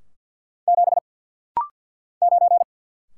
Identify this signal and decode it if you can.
Morse code, heard as a keyed tone, about 52 wpm, 700 Hz. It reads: 7 9
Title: Single Letter-Number - Rapid-Fire 50wpm